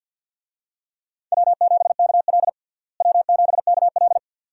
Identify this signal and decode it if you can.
Morse code, heard as a keyed tone, about 52 wpm, 700 Hz. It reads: W7KL W7KL
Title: USA Call Signs - 4 Characters - Repeated 2x - 50wpm